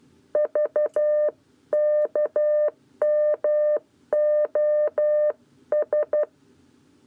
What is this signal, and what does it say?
Morse code, heard as a keyed tone, about 12 wpm, 600 Hz. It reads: VKMOS
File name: 7b lezione cw 1